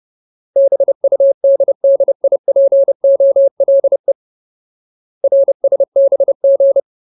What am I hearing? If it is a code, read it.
Morse code, heard as a keyed tone, about 30 wpm, 550 Hz. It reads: BUDDIPOLE RSBG
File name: Ultimate Word List / 30WPM Ultimate